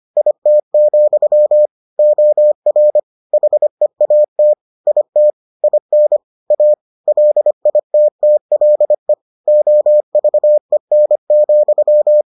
Morse code, heard as a keyed tone, about 25 wpm, 600 Hz. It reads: IT, OR HEAT IT IN A LITTLE OVEN,